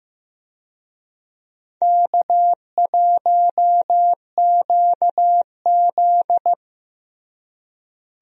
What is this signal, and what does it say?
Morse code, heard as a keyed tone, about 15 wpm, 700 Hz. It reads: K1QZ
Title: USA Call Signs - 4 Characters - Repeated 2x - 15wpm